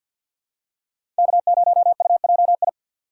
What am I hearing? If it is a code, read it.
Morse code, heard as a keyed tone, about 49 wpm, 700 Hz. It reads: K0UJI
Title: USA Call Signs - 6x Character Spacing - 50wpm